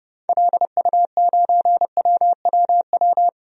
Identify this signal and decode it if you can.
Morse code, heard as a keyed tone, about 32 wpm, 700 Hz. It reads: LU9WWW
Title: DXCC-100 Call Signs - 6 Characters - Repeated 2x - 30wpm